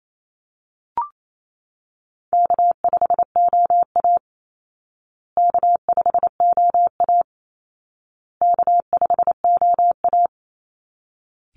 Morse code, heard as a keyed tone, about 28 wpm, 700 Hz. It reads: K5OA K5OA K5OA
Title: Call Signs Repeated 2x - 28wpm